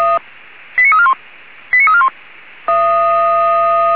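Five-tone selective call (ZVEI), heard as quick runs of five tones, tones of about 70 ms. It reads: 89341 89341
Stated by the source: ZVEI five-tone selective call received on SDR.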